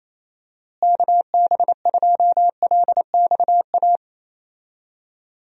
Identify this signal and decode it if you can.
Morse code, heard as a keyed tone, about 28 wpm, 700 Hz. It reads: KB2LXA
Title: USA Call Signs - 6 Characters - Repeated 3x - 28wpm